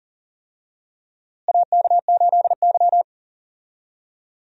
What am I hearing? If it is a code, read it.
Morse code, heard as a keyed tone, about 40 wpm, 700 Hz. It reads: AK8Y
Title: USA Call Signs - 4 Characters - Repeated 2x - 40wpm